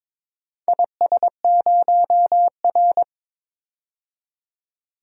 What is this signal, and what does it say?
Morse code, heard as a keyed tone, about 22 wpm, 700 Hz. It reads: IS0R
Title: DXCC-100 Call Signs - 4 Characters - 22wpm